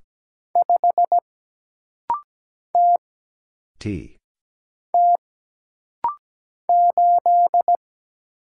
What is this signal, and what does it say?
Morse code, heard as a keyed tone, about 17 wpm, 700 Hz. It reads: 5 T T 8
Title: Single Letter-Number - Rapid-Fire 17wpm